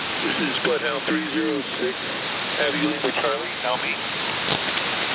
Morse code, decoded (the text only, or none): none